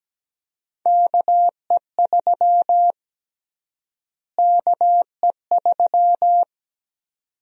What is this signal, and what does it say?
Morse code, heard as a keyed tone, about 17 wpm, 700 Hz. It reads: KE3 KE3
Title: USA Call Sign Prefixes - Repeated 2x - 17wpm